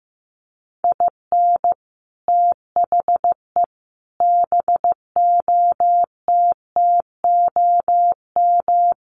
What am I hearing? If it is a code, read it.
Morse code, heard as a keyed tone, about 15 wpm, 700 Hz. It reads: IN THE BOTTOM
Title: Sets of 5 Words Encore 15wpm